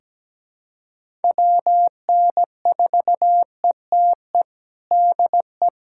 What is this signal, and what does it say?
Morse code, heard as a keyed tone, about 17 wpm, 700 Hz. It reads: WN4ETE DE
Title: USA Callsign DE Callsign 17wpm